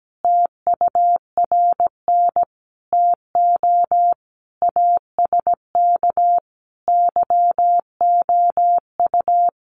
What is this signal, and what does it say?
Morse code, heard as a keyed tone, about 17 wpm, 700 Hz. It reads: TURN TO ASK YOU
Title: Sentences from Top 200 Words 17wpm